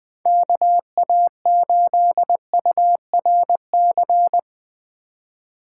Morse code, heard as a keyed tone, about 20 wpm, 700 Hz. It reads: KA8URC
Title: USA Call Signs - 6 Characters - 2x Character Spacing - 20wpm